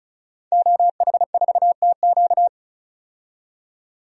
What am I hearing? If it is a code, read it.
Morse code, heard as a keyed tone, about 35 wpm, 700 Hz. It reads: OH4TQ
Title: DXCC-100 Call Signs - 5 Characters - 35wpm